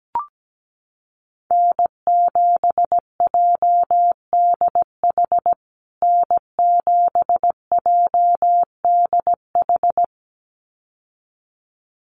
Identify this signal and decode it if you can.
Morse code, heard as a keyed tone, about 17 wpm, 700 Hz. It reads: N7JDH N7JDH
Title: USA Call Signs Repeated 2x 17wpm